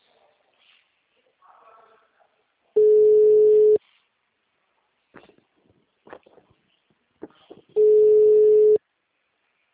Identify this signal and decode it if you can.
Ringback tone (Europe, 425 Hz).